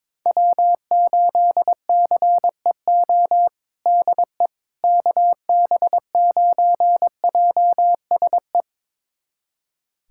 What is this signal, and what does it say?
Morse code, heard as a keyed tone, about 22 wpm, 700 Hz. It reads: W8CEO DE KB9JSE
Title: USA Callsign DE Callsign 22wpm